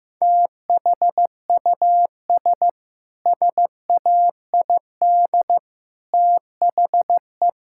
Morse code, heard as a keyed tone, about 15 wpm, 700 Hz. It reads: THUS SAID THE
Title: Sentences from Top 1000 Words - Review Part 5 - 15wpm